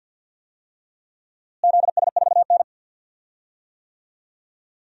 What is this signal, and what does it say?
Morse code, heard as a keyed tone, about 46 wpm, 700 Hz. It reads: ZS4N
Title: DXCC-100 Call Signs - 4 Characters - Repeated 3x - 50wpm